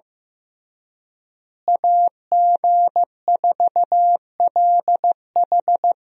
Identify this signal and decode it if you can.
Morse code, heard as a keyed tone, about 15 wpm, 700 Hz. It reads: AG4LH
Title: USA Call Signs - 5 Characters - Repeated 3x - 15wpm